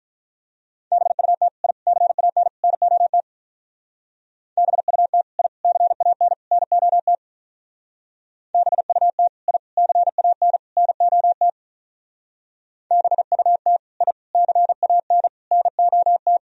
Morse code, heard as a keyed tone, about 42 wpm, 700 Hz. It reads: BUT I CAN NOT BUT I CAN NOT BUT I CAN NOT BUT I CAN NOT
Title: Speed-Racing - Sets of 4 Words 35wpm